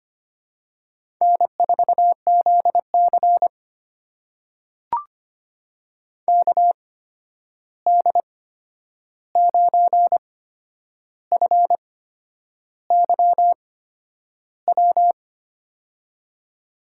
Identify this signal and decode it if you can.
Morse code, heard as a keyed tone, about 25 wpm, spaced as for 9 wpm, 700 Hz. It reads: N4ZC KD9FYW
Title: USA Call Signs - 8x Character Spacing - 25wpm